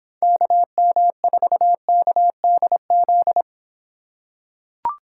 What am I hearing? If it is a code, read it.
Morse code, heard as a keyed tone, about 25 wpm, 700 Hz. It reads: KM4KDZ